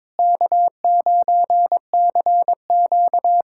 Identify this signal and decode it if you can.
Morse code, heard as a keyed tone, about 22 wpm, 700 Hz. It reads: K9CQ
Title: USA Callsign DE Callsign - 4 Characters - 22wpm